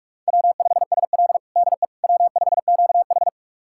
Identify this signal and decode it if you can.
Morse code, heard as a keyed tone, about 44 wpm, 690 Hz. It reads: W5SL DE W5QH
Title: USA Callsign DE Callsign - 4 Characters - 45wpm